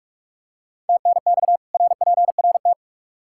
Morse code, heard as a keyed tone, about 46 wpm, 700 Hz. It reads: TNX RPRT
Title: K1USN SST QSO Practice 45wpm